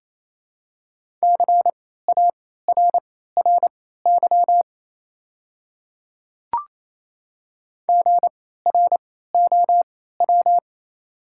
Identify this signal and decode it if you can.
Morse code, heard as a keyed tone, about 28 wpm, spaced as for 16 wpm, 700 Hz. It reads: CARRY GROW
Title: 3x Character Spacing - Top 300 Words - 28wpm